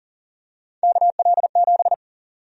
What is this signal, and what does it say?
Morse code, heard as a keyed tone, about 38 wpm, 700 Hz. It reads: KL7
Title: USA Call Sign Prefixes - Repeated 2x - 40wpm